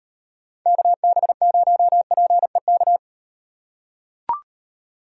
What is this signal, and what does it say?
Morse code, heard as a keyed tone, about 38 wpm, 700 Hz. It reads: KB0PEK